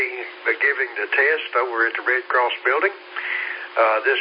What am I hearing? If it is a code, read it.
Emergency Alert System (SAME) tones, heard as harsh buzzing bursts, with speech. It is heard from 0.3 to 3.5 s.